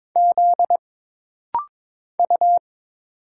Morse code, heard as a keyed tone, about 22 wpm, 700 Hz. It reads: Z U
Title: Single Letters - Rapid-Fire 22wpm